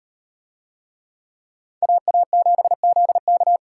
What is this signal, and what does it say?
Morse code, heard as a keyed tone, about 36 wpm, 700 Hz. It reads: AA7ZK